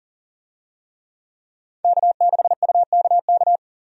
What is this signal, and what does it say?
Morse code, heard as a keyed tone, about 40 wpm, 700 Hz. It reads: K6UKK
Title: USA Call Signs - 5 Characters - 40wpm